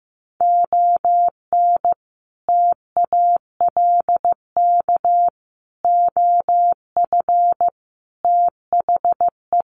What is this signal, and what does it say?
Morse code, heard as a keyed tone, about 15 wpm, 700 Hz. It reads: ON TALK OF THE
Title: Sets of 5 Words 15wpm